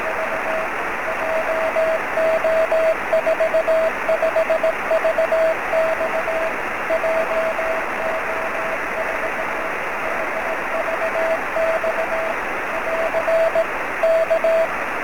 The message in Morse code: none

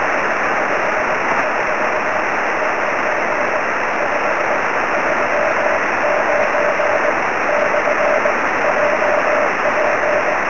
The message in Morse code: none

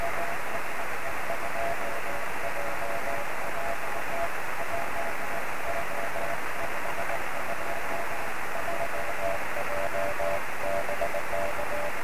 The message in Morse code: none

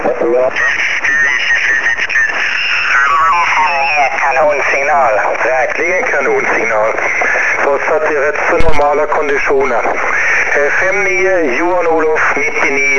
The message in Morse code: none

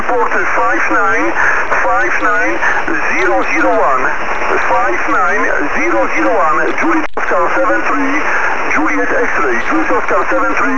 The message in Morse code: none

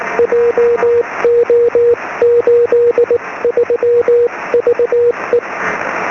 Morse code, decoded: JO83VE

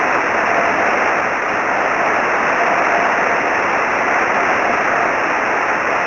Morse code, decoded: none

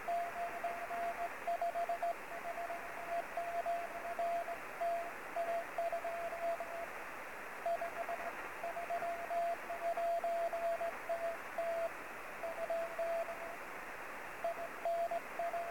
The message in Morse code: none

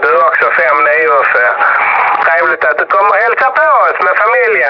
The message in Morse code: none

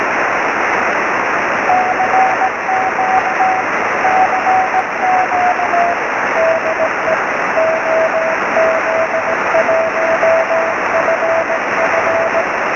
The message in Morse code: none

